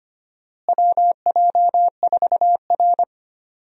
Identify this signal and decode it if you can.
Morse code, heard as a keyed tone, about 25 wpm, 700 Hz. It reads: WJ4R
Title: USA Call Signs - 4 Characters - Repeated 3x - 25wpm